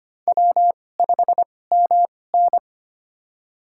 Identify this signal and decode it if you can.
Morse code, heard as a keyed tone, about 25 wpm, spaced as for 18 wpm, 700 Hz. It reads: W5MN
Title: USA Call Signs - 4 Characters - 2x Character Spacing - 25wpm